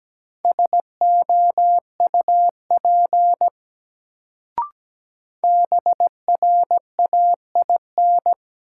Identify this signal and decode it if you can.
Morse code, heard as a keyed tone, about 17 wpm, 700 Hz. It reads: SOUP BRAIN
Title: Sentences from Top 1000 Words - Review Part 6 - New Word Drill - 17wpm